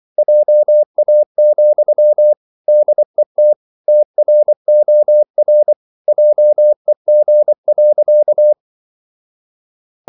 Morse code, heard as a keyed tone, about 24 wpm, 600 Hz. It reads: JA, DET TROR JEG.
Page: Ferie 24 wpm